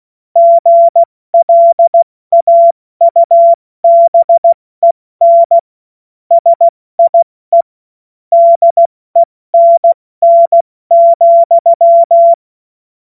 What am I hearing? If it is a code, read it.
Morse code, heard as a keyed tone, about 16 wpm, spaced as for 14 wpm, 680 Hz. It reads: GLAUBEN SIE DENN,
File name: Morsekurs / Witze / Witz 07. Juni 2022 70 F80